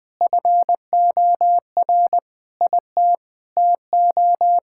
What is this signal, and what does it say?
Morse code, heard as a keyed tone, about 20 wpm, 700 Hz. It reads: FOR IT TO
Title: Sentences from Top 1000 Words - Review Part 5 - 20wpm